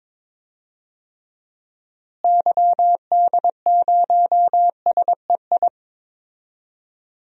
Morse code, heard as a keyed tone, about 22 wpm, 700 Hz. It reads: YD0SEI
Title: DXCC-100 Call Signs - 6 Characters - Repeated 2x - 22wpm